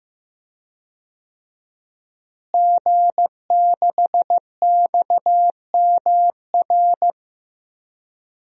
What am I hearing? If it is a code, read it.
Morse code, heard as a keyed tone, about 15 wpm, 700 Hz. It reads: G6XMR
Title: DXCC-100 Call Signs - 5 Characters - Repeated 2x - 15wpm